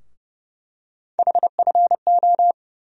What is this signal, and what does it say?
Morse code, heard as a keyed tone, about 29 wpm, 700 Hz. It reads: HFO
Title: USA Call Sign Postfixes - 3 Letters 30wpm